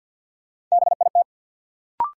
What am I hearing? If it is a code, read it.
Morse code, heard as a keyed tone, about 52 wpm, 700 Hz. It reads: BIT
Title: Sets of 3 Words - New Word Drill - 50wpm